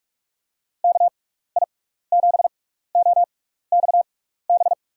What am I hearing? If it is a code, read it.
Morse code, heard as a keyed tone, about 47 wpm, spaced as for 16 wpm, 700 Hz. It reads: KI7OXB
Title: USA Call Signs - 6 Characters - 6x Character Spacing - 45wpm